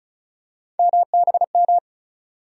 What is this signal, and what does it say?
Morse code, heard as a keyed tone, about 36 wpm, 700 Hz. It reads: MBM
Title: USA Call Sign Postfixes - 3 Letters 35wpm